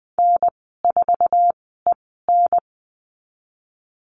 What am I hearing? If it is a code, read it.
Morse code, heard as a keyed tone, about 20 wpm, spaced as for 14 wpm, 700 Hz. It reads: N4EN
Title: 048 Lesson 9g - Number 4 - Call Signs 20wpm